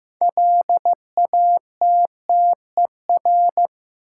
Morse code, heard as a keyed tone, about 15 wpm, 700 Hz. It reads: LATTER
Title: Sentences from Top 1600 Words - New Word Drill - 15wpm